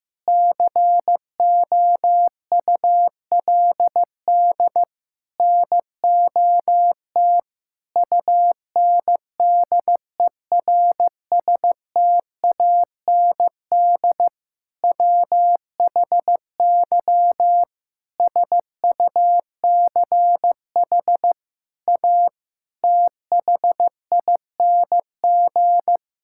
Morse code, heard as a keyed tone, about 15 wpm, 700 Hz. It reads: COULD NOT UNDERSTAND WHY SUCH A THING
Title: Sentences from Top 1000 Words - Review Part 6 - 15wpm